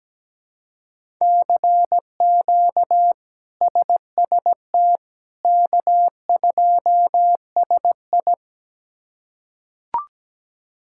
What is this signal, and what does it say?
Morse code, heard as a keyed tone, about 17 wpm, 700 Hz. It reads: CQ SST K2SI
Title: K1USN SST QSO Practice - 17wpm